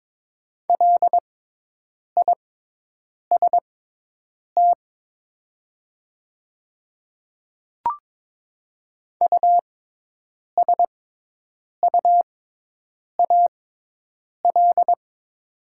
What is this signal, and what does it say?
Morse code, heard as a keyed tone, about 22 wpm, spaced as for 8 wpm, 700 Hz. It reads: LIST USUAL
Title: 6x Character Spacing - Top 500 Words - 22wpm